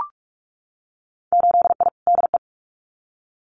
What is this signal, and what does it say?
Morse code, heard as a keyed tone, about 44 wpm, 700 Hz. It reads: 8I DE
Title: Winter Field Day Exchanges Encore 45wpm